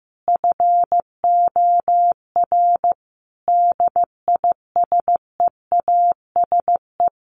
Morse code, heard as a keyed tone, about 15 wpm, 700 Hz. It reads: FOR DISEASE
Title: Sets of 5 Words Encore2 15wpm